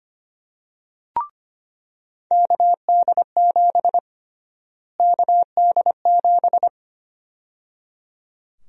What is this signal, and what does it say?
Morse code, heard as a keyed tone, about 25 wpm, 700 Hz. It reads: KD7 KD7
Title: USA Call Sign Prefixes - Repeated 2x - 25wpm